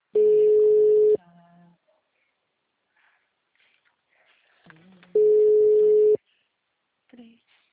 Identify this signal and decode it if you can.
Ringback tone (Europe, 425 Hz), with speech.